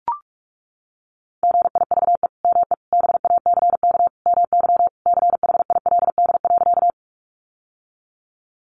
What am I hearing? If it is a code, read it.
Morse code, heard as a keyed tone, about 44 wpm, 700 Hz. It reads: GIVE ME BACK MY CHILD.
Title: Sentences from Top 800 Words 45wpm